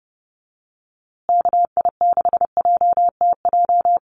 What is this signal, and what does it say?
Morse code, heard as a keyed tone, about 29 wpm, 700 Hz. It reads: KI6JTJ
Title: USA Call Signs Repeated 2x 30wpm